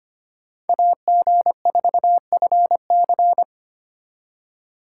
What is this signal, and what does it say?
Morse code, heard as a keyed tone, about 33 wpm, 700 Hz. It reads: AG4FC